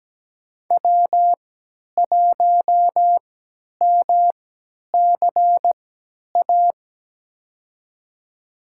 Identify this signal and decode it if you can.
Morse code, heard as a keyed tone, about 17 wpm, spaced as for 10 wpm, 700 Hz. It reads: W1MCA
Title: USA Call Signs - 5 Characters - 3x Character Spacing - 17wpm